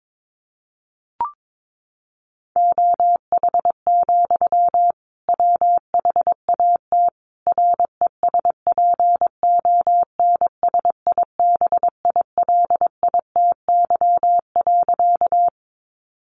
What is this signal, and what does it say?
Morse code, heard as a keyed tone, about 22 wpm, 700 Hz. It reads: OH, WHAT RESPONSIBILITY.
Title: Sentences from Top 1200 Words 22wpm